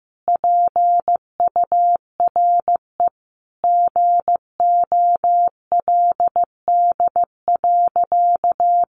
Morse code, heard as a keyed tone, about 15 wpm, 700 Hz. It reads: PURE GOLD.
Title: Sentences from Top 1500 Words 15wpm